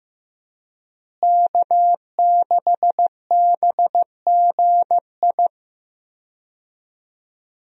Morse code, heard as a keyed tone, about 16 wpm, 700 Hz. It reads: K6BGI
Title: USA Call Signs - 5 Characters - Repeated 2x - 15wpm